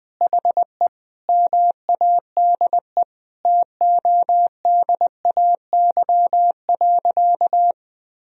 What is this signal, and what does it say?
Morse code, heard as a keyed tone, about 20 wpm, 700 Hz. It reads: HE MADE TODAY.
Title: Sentences from Top 1000 Words - Review Part 2 - 20wpm